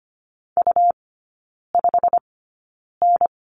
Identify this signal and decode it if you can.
Morse code, heard as a keyed tone, about 25 wpm, spaced as for 9 wpm, 700 Hz. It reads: U5N
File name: Lesson 17e - Letter u - Three Characters 25wpm